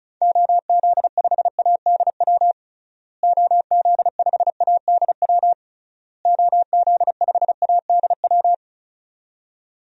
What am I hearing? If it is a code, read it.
Morse code, heard as a keyed tone, about 36 wpm, 700 Hz. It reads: OZ5ADW OZ5ADW OZ5ADW
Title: DXCC-100 Call Signs - 6 Characters - Repeated 3x - 35wpm